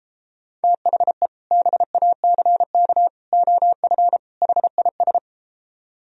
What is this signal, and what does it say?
Morse code, heard as a keyed tone, about 32 wpm, 700 Hz. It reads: THE BACK OF HIS